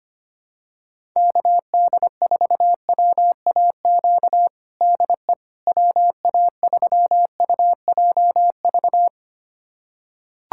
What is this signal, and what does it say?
Morse code, heard as a keyed tone, about 25 wpm, 700 Hz. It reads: KD4WAQ DE WA3UJV
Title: USA Callsign DE Callsign 25wpm